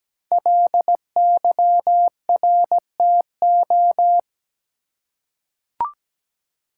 Morse code, heard as a keyed tone, about 17 wpm, 700 Hz. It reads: LYRTO
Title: Code Groups - 5 Letters - 17wpm